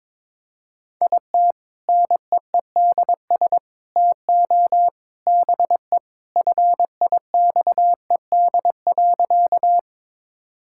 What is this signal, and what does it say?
Morse code, heard as a keyed tone, about 22 wpm, 700 Hz. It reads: IT NEEDS TO BE FIXED.
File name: practice / 22wpm / Sentences with 3-Word Phrases - 22wpm